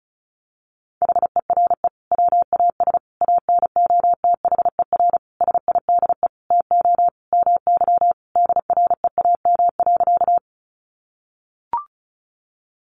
Morse code, heard as a keyed tone, about 34 wpm, 700 Hz. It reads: HERE WAS ANOTHER SIDE TO MY DREAM.
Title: Sentences from Top 1500 Words 35wpm